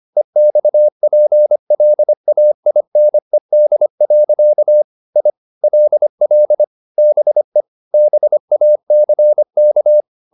Morse code, heard as a keyed tone, about 25 wpm, 600 Hz. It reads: EXPLAINED. I LL BE BACK
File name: the old man and the sea part 1 25wpm